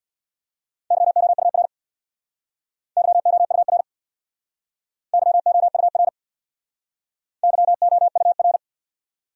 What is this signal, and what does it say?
Morse code, heard as a keyed tone, about 67 wpm, 690 Hz. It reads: YOUR YOUR YOUR YOUR